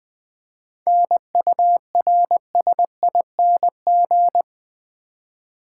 Morse code, heard as a keyed tone, about 21 wpm, 700 Hz. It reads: NURSING
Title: 7 Letter Words - First 3 Letters Given - 20wpm